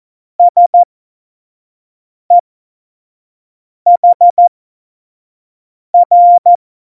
Morse code, heard as a keyed tone, about 14 wpm, spaced as for 5 wpm, 700 Hz. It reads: SEHR